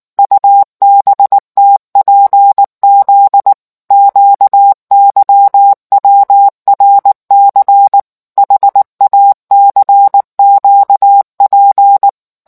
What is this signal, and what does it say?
Morse code, heard as a keyed tone, about 19 wpm, 800 Hz. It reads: UBTPZ QYWRC HACQP